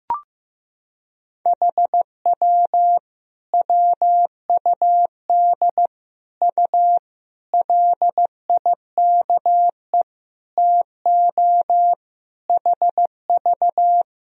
Morse code, heard as a keyed tone, about 15 wpm, 700 Hz. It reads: HW WUD U LIKE TO HV
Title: Sentences from Top 100 Words and Abbreviations 15wpm